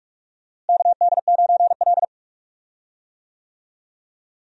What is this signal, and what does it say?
Morse code, heard as a keyed tone, about 41 wpm, 700 Hz. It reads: KD9L
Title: USA Call Signs - 4 Characters - Repeated 3x - 45wpm